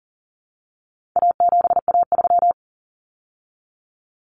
Morse code, heard as a keyed tone, about 40 wpm, 700 Hz. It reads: A7A3
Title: Common Cut Numbers - 4-Digit 40wpm